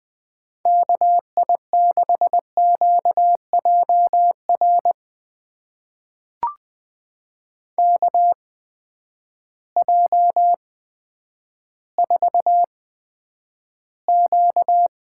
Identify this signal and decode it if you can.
Morse code, heard as a keyed tone, about 20 wpm, spaced as for 10 wpm, 700 Hz. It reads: KI6QJR KJ4Q
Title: USA Call Signs - 6 Characters - 8x Character Spacing - 20wpm